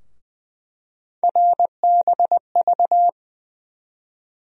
Morse code, heard as a keyed tone, about 21 wpm, 700 Hz. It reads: RBV
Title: USA Call Sign Postfixes - 3 Letters 20wpm